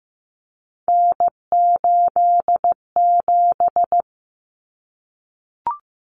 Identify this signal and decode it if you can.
Morse code, heard as a keyed tone, about 15 wpm, 700 Hz. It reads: N87
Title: Common Cut Numbers - 3-Digit 15wpm